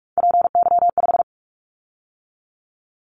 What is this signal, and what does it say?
Morse code, heard as a keyed tone, about 44 wpm, 700 Hz. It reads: PY5
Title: Callsign Prefixes for Top 50 DXCC - 45wpm